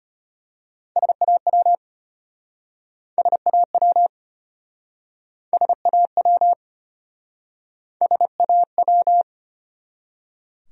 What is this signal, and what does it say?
Morse code, heard as a keyed tone, about 33 wpm, 700 Hz. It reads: SAW SAW SAW SAW